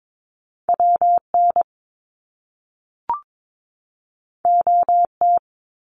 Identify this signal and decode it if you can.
Morse code, heard as a keyed tone, about 22 wpm, 700 Hz. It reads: WN OT
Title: Top Two Letter Combinations in Words 22wpm